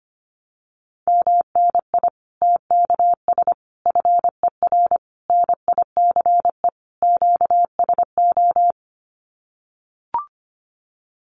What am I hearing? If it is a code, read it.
Morse code, heard as a keyed tone, about 25 wpm, 700 Hz. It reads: MNI TKS FER NICE QSO
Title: Common QSO Phrases 25wpm